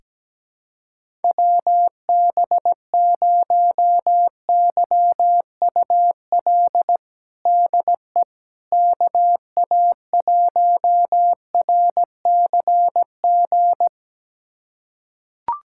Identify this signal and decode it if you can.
Morse code, heard as a keyed tone, about 17 wpm, 700 Hz. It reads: WB0YUL DE KA1RCG
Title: USA Callsign DE Callsign 17wpm